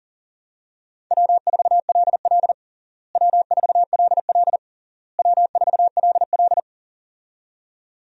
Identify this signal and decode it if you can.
Morse code, heard as a keyed tone, about 40 wpm, 700 Hz. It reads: W4LL W4LL W4LL
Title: USA Call Signs - 4 Characters - Repeated 3x - 40wpm